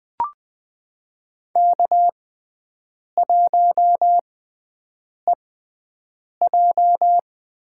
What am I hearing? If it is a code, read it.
Morse code, heard as a keyed tone, about 20 wpm, spaced as for 7 wpm, 700 Hz. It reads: K1EJ
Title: USA Call Signs - 5 Characters - 6x Character Spacing - 20wpm